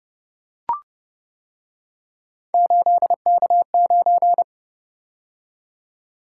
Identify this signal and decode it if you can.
Morse code, heard as a keyed tone, about 29 wpm, 700 Hz. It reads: 8K9